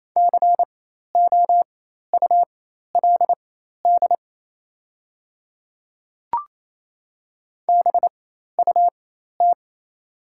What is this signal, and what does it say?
Morse code, heard as a keyed tone, about 27 wpm, spaced as for 13 wpm, 700 Hz. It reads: COULD BUT
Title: 4x Character Spacing - Top 100 Words - 28wpm